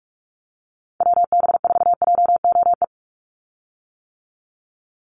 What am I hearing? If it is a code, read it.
Morse code, heard as a keyed tone, about 44 wpm, 700 Hz. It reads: WB4JOE
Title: USA Call Signs 45wpm